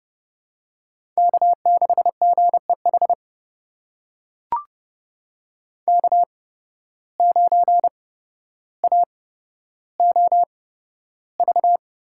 Morse code, heard as a keyed tone, about 32 wpm, spaced as for 13 wpm, 700 Hz. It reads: K6GEH K9AOV
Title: USA Call Signs - 5 Characters - 8x Character Spacing - 30wpm